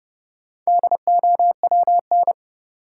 Morse code, heard as a keyed tone, about 30 wpm, 700 Hz. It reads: DOWN
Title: Sentences from Top 300 Words - New Word Drill - 30wpm